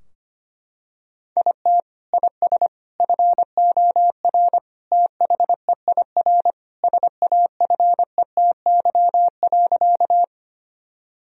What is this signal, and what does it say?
Morse code, heard as a keyed tone, about 25 wpm, 700 Hz. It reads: IT IS FOR THEIR SAFETY.
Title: Sentences from Top 1000 Words - Review Part 4 - 25wpm